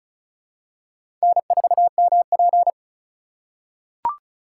Morse code, heard as a keyed tone, about 36 wpm, 700 Hz. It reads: N4MP
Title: USA Call Signs - 4x Character Spacing - 35wpm